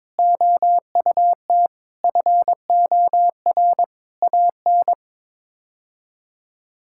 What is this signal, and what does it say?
Morse code, heard as a keyed tone, about 22 wpm, 700 Hz. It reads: OUT FOR AN
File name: practice / 22wpm / Sets of 6 Words - 22wpm